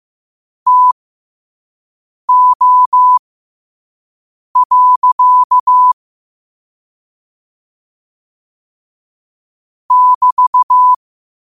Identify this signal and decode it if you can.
Morse code, heard as a keyed tone, about 15 wpm, spaced as for 5 wpm, 1000 Hz. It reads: TO. <BT>
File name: qotdfiles / 5wpm